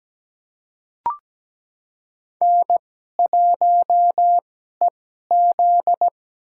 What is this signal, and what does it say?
Morse code, heard as a keyed tone, about 17 wpm, spaced as for 12 wpm, 700 Hz. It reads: N1EZ
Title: USA Call Signs - 2x Character Spacing - 17wpm